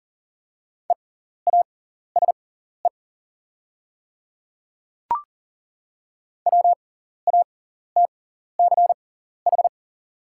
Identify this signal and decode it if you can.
Morse code, heard as a keyed tone, about 38 wpm, spaced as for 14 wpm, 690 Hz. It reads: EASE WATCH
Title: 6x Character Spacing - Top 300 Words - 40wpm